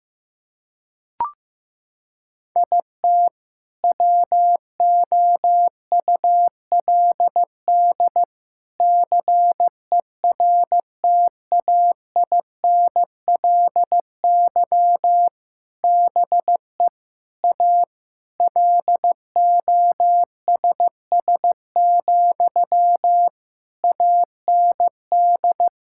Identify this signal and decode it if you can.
Morse code, heard as a keyed tone, about 16 wpm, 700 Hz. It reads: IT WOULD CERTAINLY BE A LOSS, AND
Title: Sentences from Top 1600 Words 15wpm